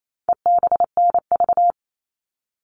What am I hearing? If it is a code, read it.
Morse code, heard as a keyed tone, about 28 wpm, 700 Hz. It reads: EBNV
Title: All Cut Numbers - 4-Digit 28wpm